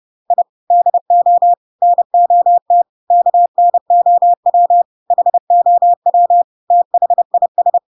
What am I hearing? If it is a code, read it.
Morse code, heard as a keyed tone, about 30 wpm, 700 Hz. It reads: I DO NOT KNOW HOW THIS
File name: Pinocchio / Pinocchio ch1 30 30 ik1hge